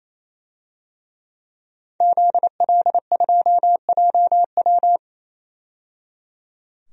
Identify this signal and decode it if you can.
Morse code, heard as a keyed tone, about 28 wpm, 700 Hz. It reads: ZL2JW
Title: DXCC-100 Call Signs - 5 Characters - 28wpm